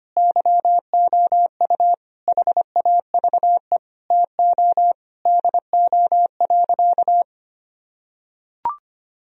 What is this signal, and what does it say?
Morse code, heard as a keyed tone, about 25 wpm, 700 Hz. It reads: YOU HAVE TO DO.
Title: Sentences from Top 1000 Words - Review Part 2 - 25wpm